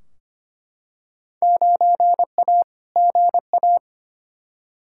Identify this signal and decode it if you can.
Morse code, heard as a keyed tone, about 25 wpm, 700 Hz. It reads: 9A GA
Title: Field Day Exchanges Encore - 25wpm